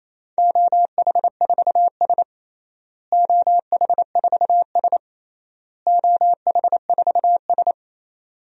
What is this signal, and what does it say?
Morse code, heard as a keyed tone, about 27 wpm, 700 Hz. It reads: OH4S OH4S OH4S
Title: DXCC-100 Call Signs - 4 Characters - Repeated 3x - 28wpm